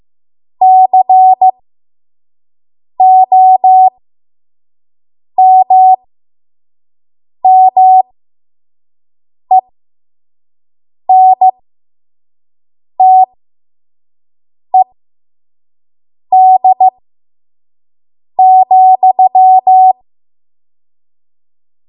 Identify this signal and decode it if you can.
Morse code, heard as a keyed tone, about 15 wpm, spaced as for 5 wpm, 750 Hz. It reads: COMMENTED,
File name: Morse / 05WPM